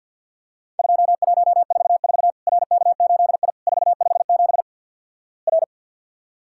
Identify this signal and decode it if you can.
Morse code, heard as a keyed tone, about 52 wpm, 700 Hz. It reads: 2144 RK8I 457 R
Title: Worked All Europe DX Contest - QTC Exchange with Repeats - 50wpm